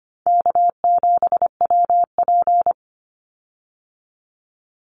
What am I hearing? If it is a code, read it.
Morse code, heard as a keyed tone, about 25 wpm, 700 Hz. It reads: K7WP
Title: Easy Call Signs Repeated 3x - 25wpm